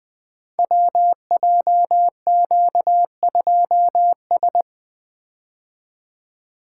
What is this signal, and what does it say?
Morse code, heard as a keyed tone, about 20 wpm, 700 Hz. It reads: WJQ2S
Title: Code Groups - 5 Letter:Numbers - 20wpm